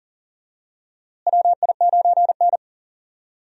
Morse code, heard as a keyed tone, about 40 wpm, 700 Hz. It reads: WI9N
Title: USA Call Signs - 4 Characters - Repeated 3x - 40wpm